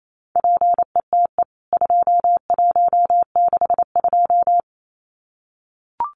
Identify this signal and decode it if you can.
Morse code, heard as a keyed tone, about 28 wpm, 700 Hz. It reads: PETE 2162
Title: CWT Exchange 28wpm